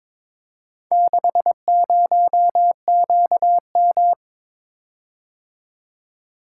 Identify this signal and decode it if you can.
Morse code, heard as a keyed tone, about 22 wpm, 700 Hz. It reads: 60QM
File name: practice / 22wpm / Four Letter-Number Combinations - 22wpm